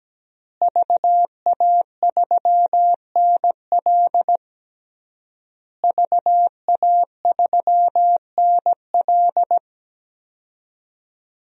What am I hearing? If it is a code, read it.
Morse code, heard as a keyed tone, about 17 wpm, 700 Hz. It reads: VA3NL VA3NL
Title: DXCC-100 Call Signs - 5 Characters - Repeated 3x - 17wpm